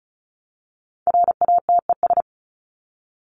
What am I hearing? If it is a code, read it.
Morse code, heard as a keyed tone, about 34 wpm, 700 Hz. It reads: RATES
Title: ES:S Suffix - Top 100 Words with Base Words 35wpm